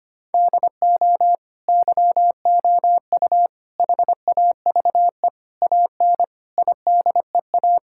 Morse code, heard as a keyed tone, about 25 wpm, 700 Hz. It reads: DO YOU HAVE AN IDEA
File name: Sets of 9 Words - 25wpm